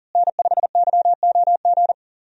Morse code, heard as a keyed tone, about 37 wpm, 700 Hz. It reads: N5YOG
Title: USA Call Signs - 5 Characters - Repeated 2x - 40wpm